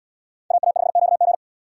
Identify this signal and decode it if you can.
Morse code, heard as a keyed tone, about 73 wpm, 700 Hz. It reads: AA5QG